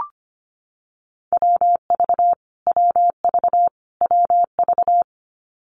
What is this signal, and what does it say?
Morse code, heard as a keyed tone, about 25 wpm, 700 Hz. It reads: WV WV WV
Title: US States Repeated 3x 25wpm